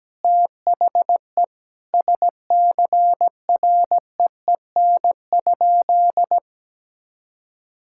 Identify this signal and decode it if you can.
Morse code, heard as a keyed tone, about 17 wpm, 700 Hz. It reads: THE SCREEN?
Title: Sentences from Top 1000 Words - Review Part 6 - 17wpm